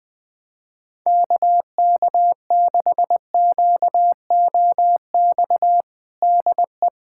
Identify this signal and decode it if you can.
Morse code, heard as a keyed tone, about 20 wpm, 700 Hz. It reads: KK6QOX DE
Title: USA Callsign DE Callsign - 6 Characters - 20wpm